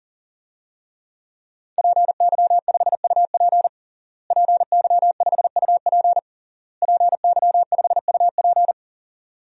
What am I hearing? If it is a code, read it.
Morse code, heard as a keyed tone, about 40 wpm, 700 Hz. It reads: PY5UP PY5UP PY5UP
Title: DXCC-100 Call Signs - 5 Characters - Repeated 3x - 40wpm